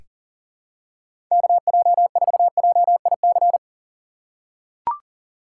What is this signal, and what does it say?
Morse code, heard as a keyed tone, about 40 wpm, 700 Hz. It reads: KJ4JIC
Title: USA Call Signs - 4x Character Spacing - 40wpm